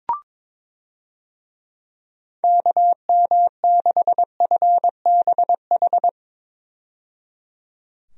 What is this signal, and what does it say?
Morse code, heard as a keyed tone, about 22 wpm, 700 Hz. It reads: KM6FBH
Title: USA Call Signs - 6 Characters - 22wpm